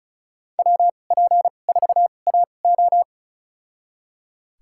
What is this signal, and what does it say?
Morse code, heard as a keyed tone, about 36 wpm, spaced as for 25 wpm, 700 Hz. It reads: WP4AO
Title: USA Call Signs - 5 Characters - 2x Character Spacing - 35wpm